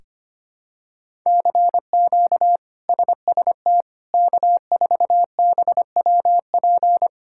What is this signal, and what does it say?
Morse code, heard as a keyed tone, about 25 wpm, 700 Hz. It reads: CQ SST K4BWP
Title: K1USN SST QSO Practice - 25wpm